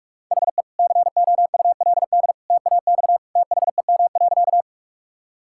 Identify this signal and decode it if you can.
Morse code, heard as a keyed tone, about 41 wpm, 700 Hz. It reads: HE COULD TAX THEM.
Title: Sentences from Top 1000 Words - Review Part 5 - 45wpm